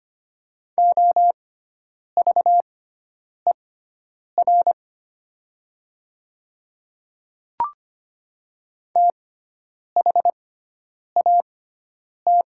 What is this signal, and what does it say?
Morse code, heard as a keyed tone, about 25 wpm, spaced as for 9 wpm, 700 Hz. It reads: OVER THAT
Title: 6x Character Spacing - Top 100 Words - 25wpm